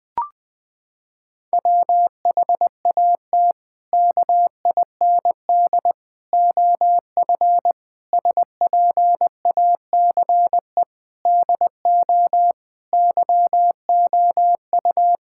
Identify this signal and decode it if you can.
Morse code, heard as a keyed tone, about 21 wpm, 700 Hz. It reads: WHAT KIND OF SPACE DO YOU
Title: Sentences from Top 1000 Words - Review Part 5 - 20wpm